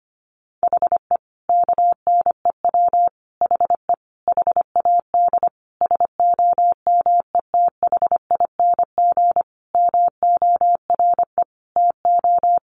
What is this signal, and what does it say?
Morse code, heard as a keyed tone, about 25 wpm, 700 Hz. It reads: HE KNEW HE HAD SOMETHING MORE TO
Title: Sentences from Top 600 Words 25wpm